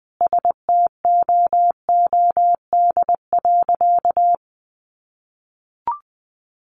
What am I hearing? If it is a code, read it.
Morse code, heard as a keyed tone, about 20 wpm, 700 Hz. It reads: STOOD.
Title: Sentences from Top 500 Words 20wpm